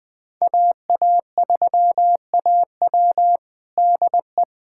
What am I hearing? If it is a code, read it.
Morse code, heard as a keyed tone, about 20 wpm, 700 Hz. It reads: AA3AW DE
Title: USA Callsign DE Callsign - 5 Characters - 20wpm